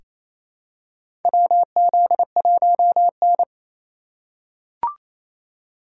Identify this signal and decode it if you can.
Morse code, heard as a keyed tone, about 27 wpm, 700 Hz. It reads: WZ1N
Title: USA Call Signs - 4 Characters - 3x Character Spacing - 28wpm